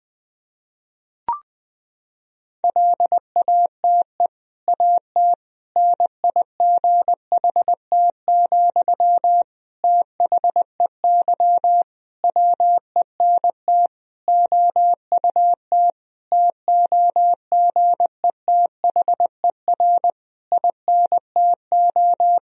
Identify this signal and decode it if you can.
Morse code, heard as a keyed tone, about 20 wpm, 700 Hz. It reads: LATE AT NIGHT, THEY WENT OUT TOGETHER INTO
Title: Sentences from Top 1000 Words - Review Part 3 - 20wpm